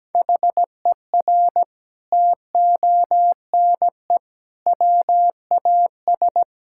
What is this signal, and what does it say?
Morse code, heard as a keyed tone, about 17 wpm, 700 Hz. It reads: HER TONE WAS
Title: Sentences from Top 1600 Words 17wpm